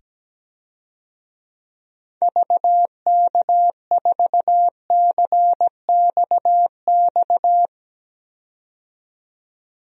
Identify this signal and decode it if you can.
Morse code, heard as a keyed tone, about 17 wpm, 700 Hz. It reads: VK4CXX
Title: DXCC-100 Call Signs - 6 Characters - 17wpm